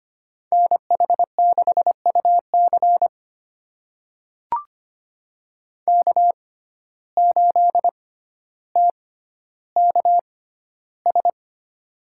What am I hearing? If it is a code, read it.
Morse code, heard as a keyed tone, about 25 wpm, spaced as for 13 wpm, 700 Hz. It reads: NH6UC K8TKS
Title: USA Call Signs - 6x Character Spacing - 25wpm